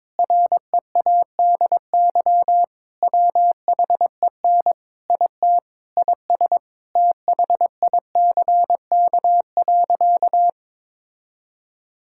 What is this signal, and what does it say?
Morse code, heard as a keyed tone, about 21 wpm, 700 Hz. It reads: READY WHEN IT IS THICK.
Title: Sentences from Top 1600 Words 22wpm